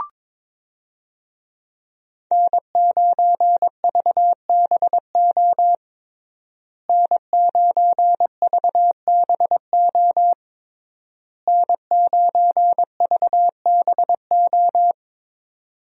Morse code, heard as a keyed tone, about 22 wpm, 700 Hz. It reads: N9VBO N9VBO N9VBO
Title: USA Call Signs - 5 Characters - Repeated 3x - 22wpm